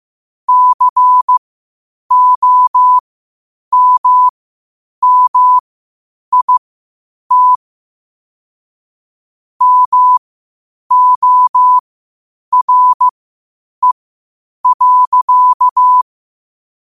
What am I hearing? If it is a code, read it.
Morse code, heard as a keyed tone, about 15 wpm, spaced as for 8 wpm, 1000 Hz. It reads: COMMIT MORE.
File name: qotdfiles / 8wpm